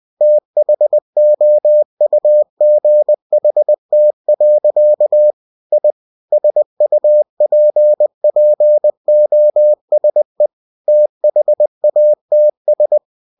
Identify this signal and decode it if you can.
Morse code, heard as a keyed tone, about 20 wpm, 600 Hz. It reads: THOUGHT. I SUPPOSE THATS
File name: cw-books / winnie-the-pooh / winnie-20wpm-page-0036